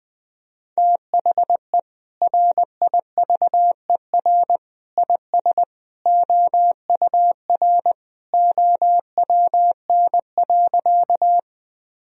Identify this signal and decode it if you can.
Morse code, heard as a keyed tone, about 20 wpm, 700 Hz. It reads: THE RIVER IS OUR OWN.
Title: Sentences from Top 1000 Words - Review Part 1 - 20wpm